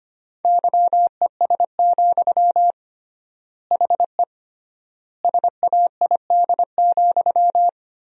Morse code, heard as a keyed tone, about 25 wpm, 700 Hz. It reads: YES, HE SAID,
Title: 3x Word-Spacing - Sentences from Top 500 Words - 25wpm